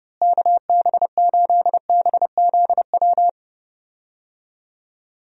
Morse code, heard as a keyed tone, about 31 wpm, 700 Hz. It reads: KB8BZW
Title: USA Call Signs - 6 Characters - Repeated 3x - 30wpm